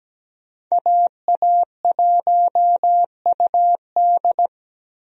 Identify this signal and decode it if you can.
Morse code, heard as a keyed tone, about 17 wpm, 700 Hz. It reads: AA1UD
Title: USA Call Signs - 5 Characters - Repeated 2x - 17wpm